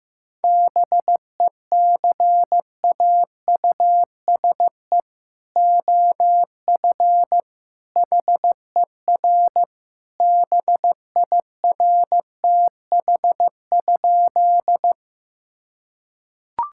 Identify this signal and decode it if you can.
Morse code, heard as a keyed tone, about 15 wpm, 700 Hz. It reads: BECAUSE OF HER BIRTH?
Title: Sentences from Top 1000 Words - Review Part 5 - 15wpm